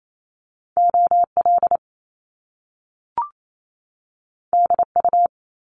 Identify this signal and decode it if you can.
Morse code, heard as a keyed tone, about 28 wpm, 700 Hz. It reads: OL DU
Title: Top Two Letter Combinations in Words 28wpm